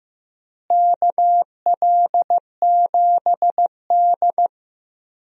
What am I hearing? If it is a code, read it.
Morse code, heard as a keyed tone, about 15 wpm, 700 Hz. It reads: KL7D
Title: DXCC-100 Call Signs - 4 Characters - Repeated 3x - 15wpm